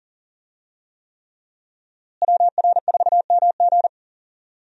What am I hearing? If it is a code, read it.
Morse code, heard as a keyed tone, about 38 wpm, 700 Hz. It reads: WR4MG
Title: USA Call Signs - 5 Characters - Repeated 3x - 40wpm